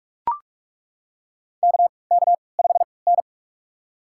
Morse code, heard as a keyed tone, about 42 wpm, spaced as for 25 wpm, 700 Hz. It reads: KK5N
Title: USA Call Signs - 3x Character Spacing - 45wpm